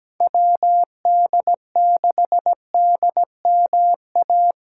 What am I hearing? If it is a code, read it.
Morse code, heard as a keyed tone, about 21 wpm, 700 Hz. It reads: WD6DMA
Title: Speed Racing - USA 2x3 Call Signs 17wpm